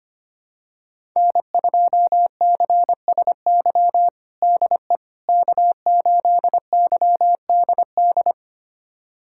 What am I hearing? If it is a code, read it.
Morse code, heard as a keyed tone, about 25 wpm, 700 Hz. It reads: N2CSY DE K8YDD
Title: USA Callsign DE Callsign - 5 Characters - 25wpm